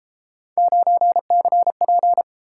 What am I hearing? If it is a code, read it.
Morse code, heard as a keyed tone, about 31 wpm, 700 Hz. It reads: 9CP